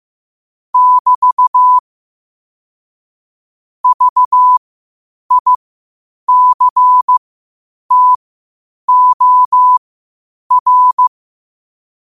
Morse code, heard as a keyed tone, about 15 wpm, spaced as for 8 wpm, 1000 Hz. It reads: <BT> VICTOR
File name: qotdfiles / 8wpm